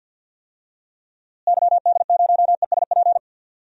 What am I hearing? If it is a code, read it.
Morse code, heard as a keyed tone, about 51 wpm, 700 Hz. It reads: YD0ESP
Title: DXCC-100 Call Signs - 6 Characters - Repeated 2x - 50wpm